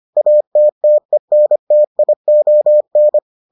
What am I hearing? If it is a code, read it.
Morse code, heard as a keyed tone, about 25 wpm, 600 Hz. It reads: ATTENTION